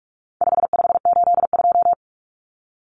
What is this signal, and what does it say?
Morse code, heard as a keyed tone, about 44 wpm, 700 Hz. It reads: 5582
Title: Numbers - 4-Digit 45wpm